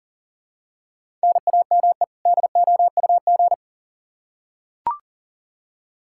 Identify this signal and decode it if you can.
Morse code, heard as a keyed tone, about 38 wpm, 700 Hz. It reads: NAME DOUG
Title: Practice QSO - Exchange 1 - Beginning - 40wpm